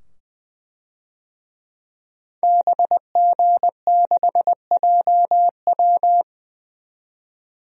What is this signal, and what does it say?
Morse code, heard as a keyed tone, about 20 wpm, 700 Hz. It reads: BG6JW
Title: DXCC-100 Call Signs - 5 Characters - 20wpm